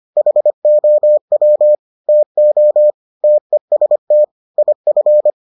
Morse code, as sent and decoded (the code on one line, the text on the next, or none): .... --- .-- / - --- / - . ... - / .. ..-.
HOW TO TEST IF